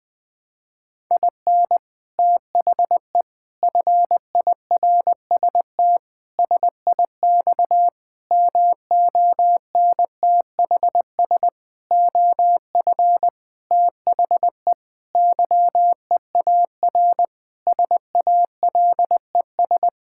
Morse code, heard as a keyed tone, about 20 wpm, 700 Hz. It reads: IN THE FIRST SIX MONTHS OF THE YEAR SALES
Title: Sentences with 6-Word Phrases - Spoken Beginning - 20wpm